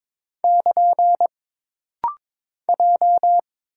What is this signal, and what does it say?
Morse code, heard as a keyed tone, about 22 wpm, 700 Hz. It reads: <KN> J
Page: This rapid-fire timing is geared to push your ICR (Instant Character Recognition) skills.